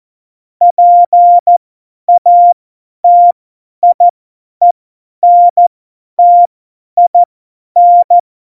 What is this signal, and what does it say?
Morse code, heard as a keyed tone, about 14 wpm, spaced as for 10 wpm, 700 Hz. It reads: PATIENTIN